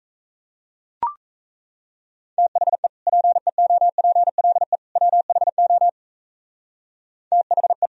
Morse code, heard as a keyed tone, about 37 wpm, 700 Hz. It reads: THE PEOPLE WHO THE